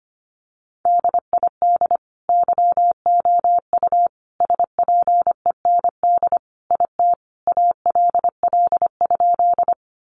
Morse code, heard as a keyed tone, about 25 wpm, 700 Hz. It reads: DID YOU SPEND IT ALL?
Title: Sentences from Top 800 Words 25wpm